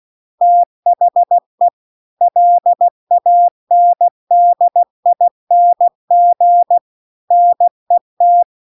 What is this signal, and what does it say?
Morse code, heard as a keyed tone, about 16 wpm, 700 Hz. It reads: THE LANDING NET
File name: CW-The-Big-Twoheaded-River-2-16WPM